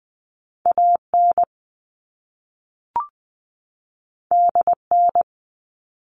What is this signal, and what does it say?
Morse code, heard as a keyed tone, about 20 wpm, 700 Hz. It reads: AN DN
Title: All Cut Numbers - 2-Digit 20wpm